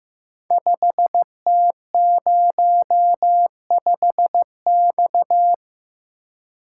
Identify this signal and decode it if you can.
Morse code, heard as a keyed tone, about 15 wpm, 700 Hz. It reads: 5T05X
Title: Code Groups - 5 Letter:Numbers - 15wpm